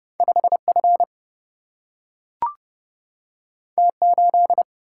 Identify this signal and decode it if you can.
Morse code, heard as a keyed tone, about 30 wpm, 700 Hz. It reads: 5F T8
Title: Single-Letter Single-Number 30wpm